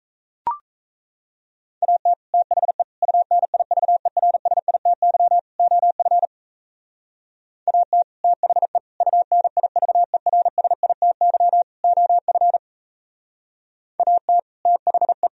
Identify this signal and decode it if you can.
Morse code, heard as a keyed tone, about 35 wpm, 700 Hz. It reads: AT THE UNIVERSITY OF AT THE UNIVERSITY OF AT THE